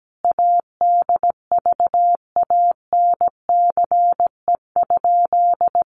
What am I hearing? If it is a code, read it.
Morse code, heard as a keyed tone, about 17 wpm, 700 Hz. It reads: ADVANCE?
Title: Sentences from Top 1400 Words 17wpm